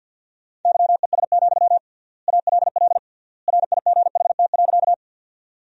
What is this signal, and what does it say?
Morse code, heard as a keyed tone, about 54 wpm, 700 Hz. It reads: YES, ALL RIGHT.
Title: 3x Word-Spacing - Sentences from Top 500 Words - 50wpm